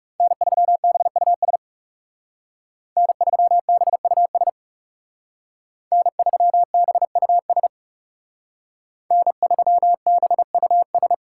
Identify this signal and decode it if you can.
Morse code, heard as a keyed tone, about 36 wpm, 700 Hz. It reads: N3BUS N3BUS N3BUS N3BUS
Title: Speed Racing - USA Call Signs 30wpm